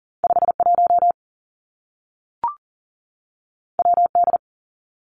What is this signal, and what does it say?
Morse code, heard as a keyed tone, about 40 wpm, 700 Hz. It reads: 51 WD
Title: Two Challenging Letter-Number Combinations 40wpm